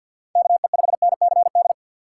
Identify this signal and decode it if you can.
Morse code, heard as a keyed tone, about 46 wpm, 700 Hz. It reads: KE5NCD
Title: USA Call Signs - 6 Characters - 3x Character Spacing - 50wpm